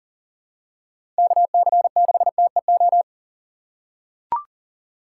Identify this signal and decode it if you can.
Morse code, heard as a keyed tone, about 40 wpm, 700 Hz. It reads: KC6TEO
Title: USA Call Signs - 4x Character Spacing - 40wpm